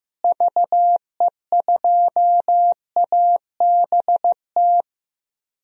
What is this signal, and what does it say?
Morse code, heard as a keyed tone, about 15 wpm, 700 Hz. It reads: VE2ABT
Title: DXCC-100 Call Signs - 6 Characters - Repeated 3x - 15wpm